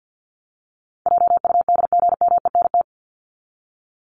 Morse code, heard as a keyed tone, about 50 wpm, 700 Hz. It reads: JUDGMENT
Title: Top 1751-2000 Words 50wpm